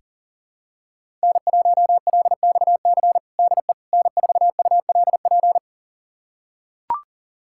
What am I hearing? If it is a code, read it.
Morse code, heard as a keyed tone, about 40 wpm, 700 Hz. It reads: N1LXC DE N4ULP
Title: USA Callsign DE Callsign - 5 Characters - 40wpm